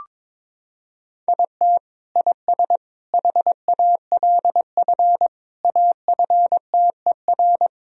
Morse code, heard as a keyed tone, about 22 wpm, 700 Hz. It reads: IT IS HALF AFTER
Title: Sentences from Top 1000 Words - Review Part 1 - 22wpm